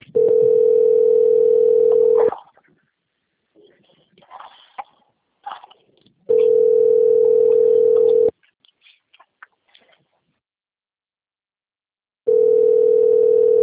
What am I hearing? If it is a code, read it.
Ringback tone (North America), 440 + 480 Hz.